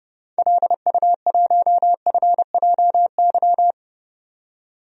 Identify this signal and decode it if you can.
Morse code, heard as a keyed tone, about 30 wpm, 700 Hz. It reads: LU1FJY
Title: DXCC-100 Call Signs - 6 Characters - 30wpm